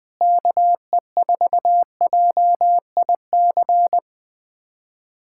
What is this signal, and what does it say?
Morse code, heard as a keyed tone, about 20 wpm, 700 Hz. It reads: KE4JIC
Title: USA Callsign DE Callsign - 6 Characters - 20wpm